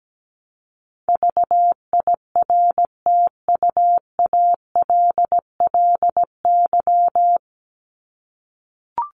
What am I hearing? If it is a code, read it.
Morse code, heard as a keyed tone, about 17 wpm, 700 Hz. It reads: VIRTUALLY
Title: LY Suffix - Top 100 Words with Base Words 17wpm